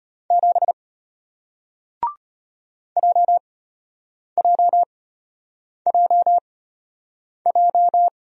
Morse code, heard as a keyed tone, about 34 wpm, spaced as for 8 wpm, 700 Hz. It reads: 7 JJJJ